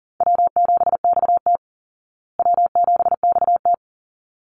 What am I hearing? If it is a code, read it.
Morse code, heard as a keyed tone, about 40 wpm, 700 Hz. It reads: W7XT W7XT
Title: Easy Call Signs Repeated 2x - 40wpm